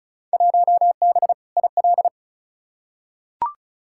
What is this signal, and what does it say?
Morse code, heard as a keyed tone, about 36 wpm, 700 Hz. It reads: 1B IL
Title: Field Day Exchanges - 35wpm